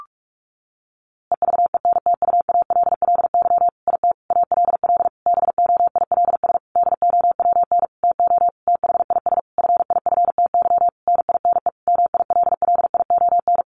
Morse code, heard as a keyed tone, about 44 wpm, 700 Hz. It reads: EVENTUALLY IT ALL BOILS DOWN TO THIS FIFTY NINE MILLION
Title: A Sound Of Thunder 45wpm